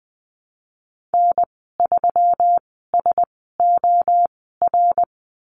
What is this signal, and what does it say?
Morse code, heard as a keyed tone, about 20 wpm, spaced as for 14 wpm, 700 Hz. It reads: N3SOR
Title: 161 Lesson 26g - Letter g - Call Signs 20wpm